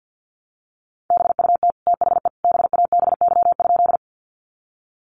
Morse code, heard as a keyed tone, about 50 wpm, 700 Hz. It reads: BUT THE BABY?
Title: Sentences from Top 1300 Words 50wpm